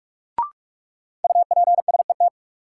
Morse code, heard as a keyed tone, about 46 wpm, 700 Hz. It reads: UPSET
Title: Sentences from Top 800 Words - New Word Drill - 45wpm